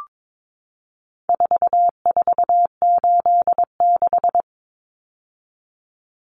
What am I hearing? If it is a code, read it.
Morse code, heard as a keyed tone, about 22 wpm, 700 Hz. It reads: 4486
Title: Numbers - 4-Digit 22wpm